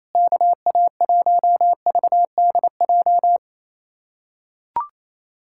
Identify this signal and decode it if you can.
Morse code, heard as a keyed tone, about 28 wpm, 700 Hz. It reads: KA1VDJ
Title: USA Callsign DE Callsign - 6 Characters - 28wpm